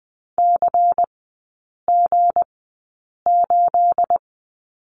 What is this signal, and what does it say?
Morse code, heard as a keyed tone, about 20 wpm, spaced as for 8 wpm, 690 Hz. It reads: CG8
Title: 212 Lesson 34e - Number 8 - Three Characters 20wpm